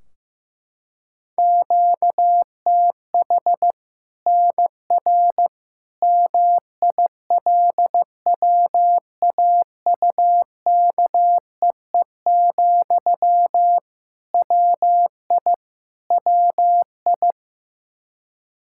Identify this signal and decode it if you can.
Morse code, heard as a keyed tone, about 15 wpm, 700 Hz. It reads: QTH NR MILWAUKEE, WI WI
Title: K1USN SST QSO Practice 15wpm